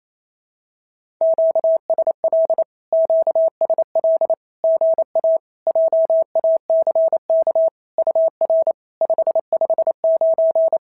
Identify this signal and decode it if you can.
Morse code, heard as a keyed tone, about 28 wpm, 690 Hz. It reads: QSL QSL GA JACK UR 559
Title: SOTA QSO Practice 28wpm